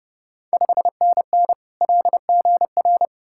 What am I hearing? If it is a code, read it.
Morse code, heard as a keyed tone, about 30 wpm, 700 Hz. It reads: 5NN LGR
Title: Ontario QSO Party Exchange 30wpm